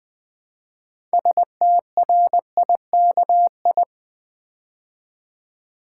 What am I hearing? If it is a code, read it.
Morse code, heard as a keyed tone, about 20 wpm, 700 Hz. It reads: STRIKI
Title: Build A Word - 2 Letters At A Time - 8-Letter Words - 20wpm